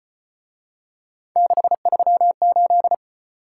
Morse code, heard as a keyed tone, about 34 wpm, 700 Hz. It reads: 638